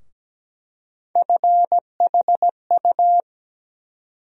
Morse code, heard as a keyed tone, about 17 wpm, 700 Hz. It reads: FHU
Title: USA Call Sign Postfixes - 3 Letters 17wpm